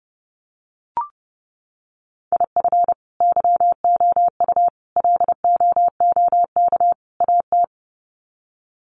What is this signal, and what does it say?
Morse code, heard as a keyed tone, about 30 wpm, 700 Hz. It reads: IF YOU LOOK AT
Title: Sets of 4 Words 30wpm